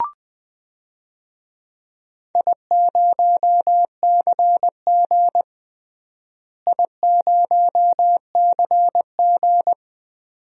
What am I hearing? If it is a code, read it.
Morse code, heard as a keyed tone, about 20 wpm, 700 Hz. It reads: I0CG I0CG
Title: DXCC-100 Call Signs - 4 Characters - Repeated 2x - 20wpm